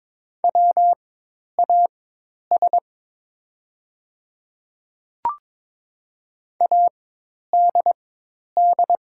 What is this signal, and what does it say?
Morse code, heard as a keyed tone, about 22 wpm, spaced as for 10 wpm, 690 Hz. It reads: WAS ADD
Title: 4x Character Spacing - Top 200 Words - 22wpm